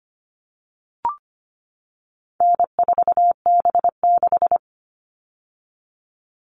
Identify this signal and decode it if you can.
Morse code, heard as a keyed tone, about 25 wpm, 700 Hz. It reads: N4B6
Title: All Cut Numbers - 4-Digit 25wpm